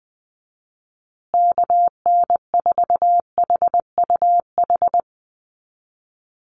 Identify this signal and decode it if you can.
Morse code, heard as a keyed tone, about 20 wpm, 700 Hz. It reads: KN4HUH
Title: USA Call Signs Repeated 3x 20wpm